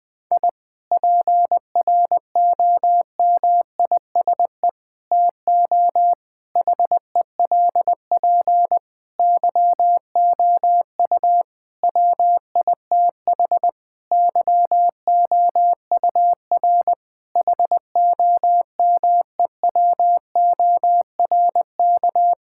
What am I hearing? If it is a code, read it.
Morse code, heard as a keyed tone, about 20 wpm, 700 Hz. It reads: I PROMISE TO HELP YOU WITH YOUR HOMEWORK
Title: Difficult Sentences with Hints - 20wpm